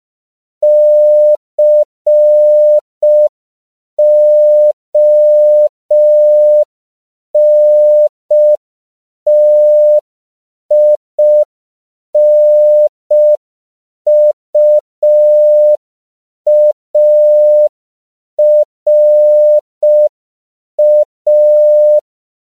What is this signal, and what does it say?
Morse code, heard as a keyed tone, about 5 wpm, 600 Hz. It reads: CONTINUARA